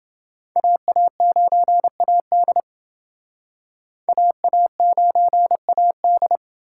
Morse code, heard as a keyed tone, about 28 wpm, 700 Hz. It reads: AA9AD AA9AD